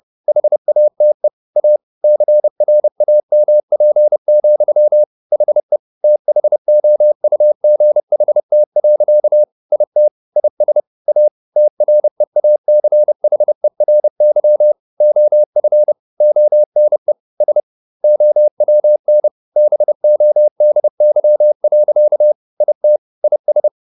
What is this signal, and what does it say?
Morse code, heard as a keyed tone, about 30 wpm, 600 Hz. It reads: HATE A CRAMP, HE THOUGHT. IT IS A TREACHERY OF ONE S OWN BODY. IT IS
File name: the old man and the sea / the old man and the sea 1 30wpm